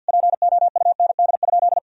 Morse code, heard as a keyed tone, about 50 wpm, 700 Hz. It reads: POUND?
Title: Sentences from Top 1000 Words - Review Part 2 - 50wpm